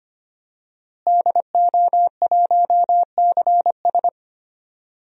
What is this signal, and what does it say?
Morse code, heard as a keyed tone, about 25 wpm, 700 Hz. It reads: DO1CS
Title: DXCC-100 Call Signs - 5 Characters - Repeated 3x - 25wpm